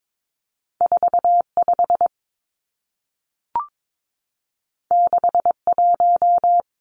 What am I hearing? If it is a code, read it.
Morse code, heard as a keyed tone, about 22 wpm, 700 Hz. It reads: 45 61
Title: Numbers - 2-Digit 22wpm